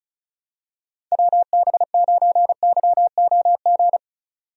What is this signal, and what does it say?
Morse code, heard as a keyed tone, about 36 wpm, 700 Hz. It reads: WB9YOG
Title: USA Call Signs - 6 Characters - 35wpm